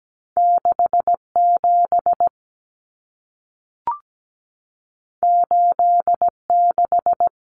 Morse code, heard as a keyed tone, about 17 wpm, 700 Hz. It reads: 67 86
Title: Numbers - 2-Digit 17wpm